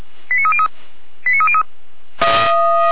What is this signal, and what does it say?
Five-tone selective call (ZVEI), heard as quick runs of five tones, tones of about 70 ms. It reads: 89303 89303